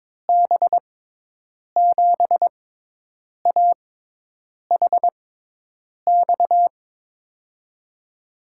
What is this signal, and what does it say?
Morse code, heard as a keyed tone, about 22 wpm, spaced as for 8 wpm, 700 Hz. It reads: B7AHX
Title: USA Call Signs - 6 Characters - 6x Character Spacing - 22wpm